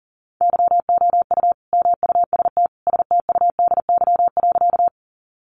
Morse code, heard as a keyed tone, about 40 wpm, 700 Hz. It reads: YOU MUST STUDY.
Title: Sentences from Top 300 Words 40wpm